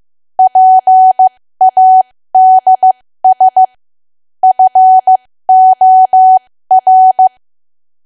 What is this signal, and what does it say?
Morse code, heard as a keyed tone, about 15 wpm, spaced as for 13 wpm, 750 Hz. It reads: PADS FOR
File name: cwa / W1AW-Level2 / 13WPM